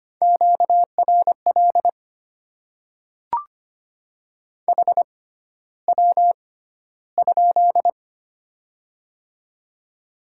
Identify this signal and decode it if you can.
Morse code, heard as a keyed tone, about 25 wpm, spaced as for 13 wpm, 700 Hz. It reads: QRL HW?
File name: Top 20 QSO Elements - 6x Character Spacing - 25wpm